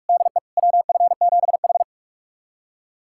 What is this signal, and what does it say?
Morse code, heard as a keyed tone, about 41 wpm, 700 Hz. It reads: DE WF7H
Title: USA Callsign DE Callsign - 4 Characters - 45wpm